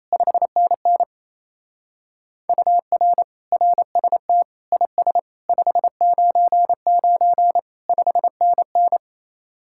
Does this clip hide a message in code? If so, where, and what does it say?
Yes: Morse code from 0.1 to 9.0 s, heard as a keyed tone, about 36 wpm, 700 Hz. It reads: 5NN UR RST IS 599 5NN